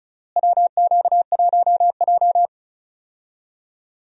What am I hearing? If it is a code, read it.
Morse code, heard as a keyed tone, about 34 wpm, 700 Hz. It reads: WQ1J
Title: USA Call Signs - 4 Characters - Repeated 3x - 35wpm